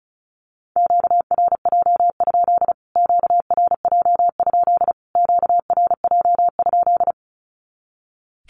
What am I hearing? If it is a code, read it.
Morse code, heard as a keyed tone, about 34 wpm, 700 Hz. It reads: QRJ? QRJ? QRJ?
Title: Q-Codes Repeated 3x 35wpm